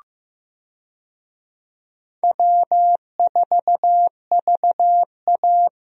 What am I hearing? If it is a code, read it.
Morse code, heard as a keyed tone, about 15 wpm, 700 Hz. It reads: W4VA
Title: USA Call Signs - 4 Characters - 15wpm